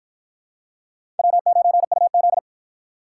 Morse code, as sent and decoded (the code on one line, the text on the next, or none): .-- ----. ..- --..
W9UZ